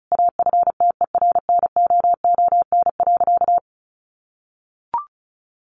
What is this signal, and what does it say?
Morse code, heard as a keyed tone, about 34 wpm, 700 Hz. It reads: AFTERNOON.
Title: Sentences from Top 1500 Words 35wpm